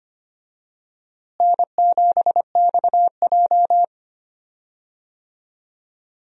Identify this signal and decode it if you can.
Morse code, heard as a keyed tone, about 25 wpm, 700 Hz. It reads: N7XJ
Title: USA Call Signs - 4 Characters - Repeated 3x - 25wpm